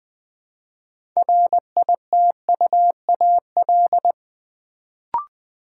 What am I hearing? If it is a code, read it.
Morse code, heard as a keyed tone, about 20 wpm, 700 Hz. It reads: RITUAL
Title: Build A Word - 3 Letters At A Time - 6-Letter Words - 20wpm